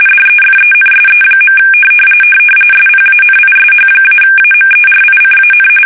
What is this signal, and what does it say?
RTTY, about 45.45 baud. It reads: DEFENSE. SPANISH-AMERICAN
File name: fsk850 / usnavy3